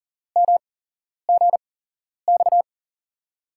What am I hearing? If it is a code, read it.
Morse code, heard as a keyed tone, about 40 wpm, 700 Hz. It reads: M G X
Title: USA Call Signs - 6 Characters - 8x Character Spacing - 40wpm